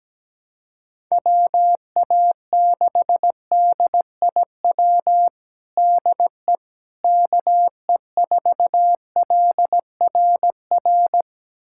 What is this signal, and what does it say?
Morse code, heard as a keyed tone, about 17 wpm, 700 Hz. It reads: WA6DIW DE KE4LRR
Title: USA Callsign DE Callsign 17wpm